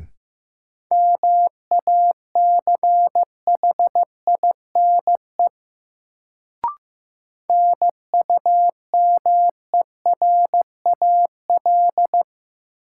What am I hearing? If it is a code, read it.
Morse code, heard as a keyed tone, about 15 wpm, 700 Hz. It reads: MACHINE NUMERAL
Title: Sentences from Top 1000 Words - Review Part 2 - New Word Drill - 15wpm